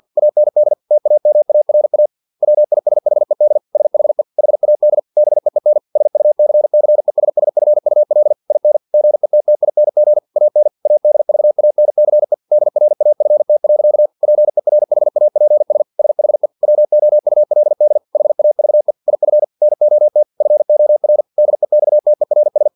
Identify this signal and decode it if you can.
Morse code, heard as a keyed tone, about 50 wpm, 600 Hz. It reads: AND TAMARA WISHED SHE HAD BEEN SUCCESSFUL IN GETTING AN ADVANCE DRAFT. PERHAPS SHE WOULD HAVE IF NOT FOR DEXTERS
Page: Never 49wpm